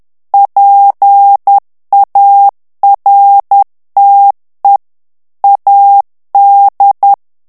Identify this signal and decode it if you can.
Morse code, heard as a keyed tone, about 10 wpm, 800 Hz. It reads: PARTE AD
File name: ITALIEN texte clair 2 vitesse 10 mots minute